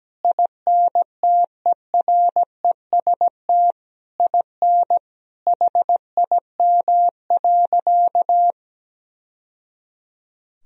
Morse code, heard as a keyed tone, about 17 wpm, 700 Hz. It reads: INTEREST IN HIM.
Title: Sentences from Top 1000 Words - Review Part 3 - 17wpm